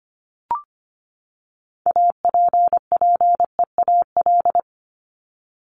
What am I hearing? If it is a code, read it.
Morse code, heard as a keyed tone, about 25 wpm, 700 Hz. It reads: APPEAL
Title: Top 1251-1500 Words 25wpm